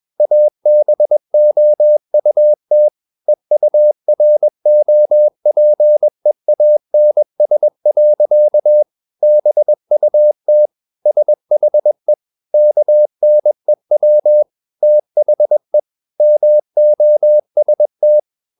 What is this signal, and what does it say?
Morse code, heard as a keyed tone, about 21 wpm, 600 Hz. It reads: ABOUT EUROPEANS. BUT SHE KNEW THE MOST
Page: CW med Gnister